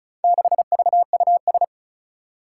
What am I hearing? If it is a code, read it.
Morse code, heard as a keyed tone, about 36 wpm, 700 Hz. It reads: 6VUS
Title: Four Challenging Letter-Number Combinations 35wpm